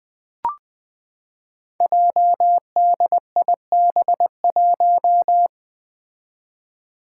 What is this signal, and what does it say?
Morse code, heard as a keyed tone, about 20 wpm, 700 Hz. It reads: JDIB1
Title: Five Challenging Letter-Number Combinations 20wpm